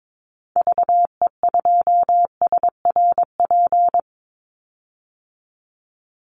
Morse code, heard as a keyed tone, about 22 wpm, 700 Hz. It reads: VE2SRP
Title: Call Signs Repeated 3x - 22wpm